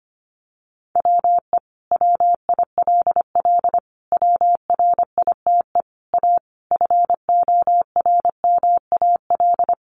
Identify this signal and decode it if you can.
Morse code, heard as a keyed tone, about 25 wpm, 700 Hz. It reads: WE WILL WRITE A FORMAL
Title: Sentences from Top 1300 Words 25wpm